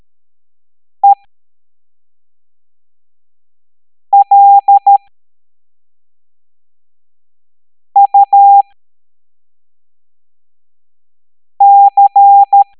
Morse code, heard as a keyed tone, about 13 wpm, spaced as for 3 wpm, 800 Hz. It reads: ELUC